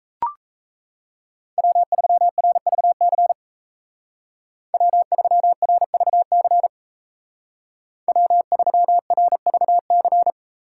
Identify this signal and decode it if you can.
Morse code, heard as a keyed tone, about 34 wpm, 700 Hz. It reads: W3RVC W3RVC W3RVC
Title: Speed Racing - USA 1x3 Call Signs 28wpm